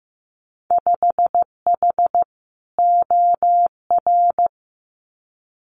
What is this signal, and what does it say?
Morse code, heard as a keyed tone, about 15 wpm, 700 Hz. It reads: 5H OR
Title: Winter Field Day Exchanges Encore 15wpm